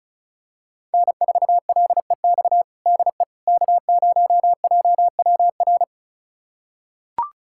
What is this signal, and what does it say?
Morse code, heard as a keyed tone, about 36 wpm, 690 Hz. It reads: N4LEX DE K0JWR
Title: USA Callsign DE Callsign - 5 Characters - 35wpm